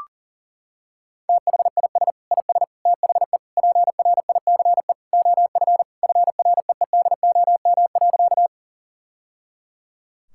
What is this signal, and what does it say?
Morse code, heard as a keyed tone, about 40 wpm, 700 Hz. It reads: THIS IS THE PRICE OF FREEDOM.
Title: Sentences from Top 1000 Words - Review Part 5 - 40wpm